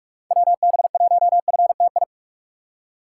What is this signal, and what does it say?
Morse code, heard as a keyed tone, about 46 wpm, 700 Hz. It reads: WB1FTI
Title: USA Call Signs - 6 Characters - 2x Character Spacing - 45wpm